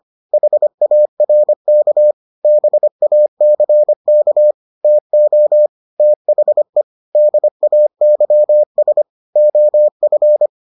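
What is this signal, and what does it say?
Morse code, heard as a keyed tone, about 25 wpm, 600 Hz. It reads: HARK BACK TO THE DAYS OF
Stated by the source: Morse edition of the Foundations of Amateur Radio episodes.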